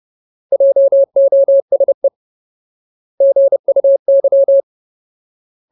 Morse code, heard as a keyed tone, about 30 wpm, 550 Hz. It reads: JOSE GUY
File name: Ultimate Word List / 30WPM Ultimate